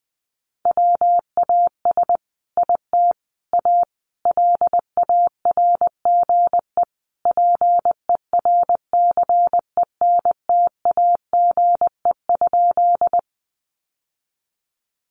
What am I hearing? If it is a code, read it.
Morse code, heard as a keyed tone, about 20 wpm, 700 Hz. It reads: WAS IT A LARGE PERCENTAGE?
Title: Sentences from Top 1300 Words 20wpm